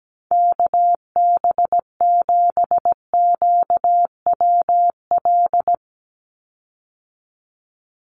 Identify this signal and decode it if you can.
Morse code, heard as a keyed tone, about 17 wpm, 700 Hz. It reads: KB7QWL
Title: USA Call Signs 17wpm